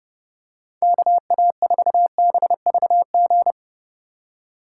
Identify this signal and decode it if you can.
Morse code, heard as a keyed tone, about 30 wpm, 700 Hz. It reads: KA4BVG
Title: K1USN SST QSO Practice - 30wpm